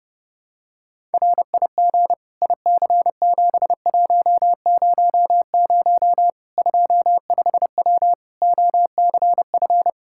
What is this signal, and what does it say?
Morse code, heard as a keyed tone, about 30 wpm, 700 Hz. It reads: RIG IC7100 25W OCF
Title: K1USN SST QSO Practice 30wpm